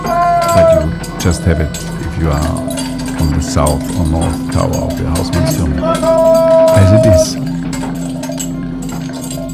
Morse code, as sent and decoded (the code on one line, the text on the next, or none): none